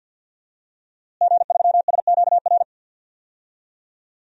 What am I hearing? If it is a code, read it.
Morse code, heard as a keyed tone, about 52 wpm, 700 Hz. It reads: G3SQR
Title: DXCC-100 Call Signs - 5 Characters - Repeated 2x - 50wpm